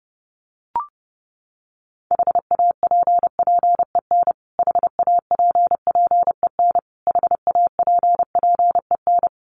Morse code, heard as a keyed tone, about 30 wpm, 700 Hz. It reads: HAPPEN HAPPEN HAPPEN
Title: Top 500 Words Repeated 30wpm